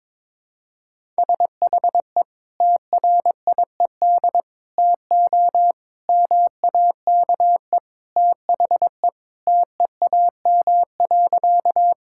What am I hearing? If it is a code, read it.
Morse code, heard as a keyed tone, about 22 wpm, 700 Hz. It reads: SHE TRIED TO MAKE THE TEAM.
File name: practice / 22wpm / Sentences with 3-Word Phrases - 22wpm